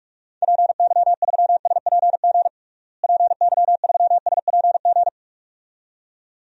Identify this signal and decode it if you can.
Morse code, heard as a keyed tone, about 46 wpm, 700 Hz. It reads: PY3SPG PY3SPG
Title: DXCC-100 Call Signs - 6 Characters - Repeated 2x - 45wpm